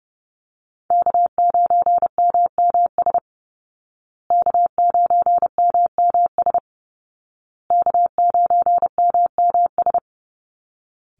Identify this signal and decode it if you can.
Morse code, heard as a keyed tone, about 30 wpm, 700 Hz. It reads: K9MMS K9MMS K9MMS
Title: Call Signs Repeated 2x - 30wpm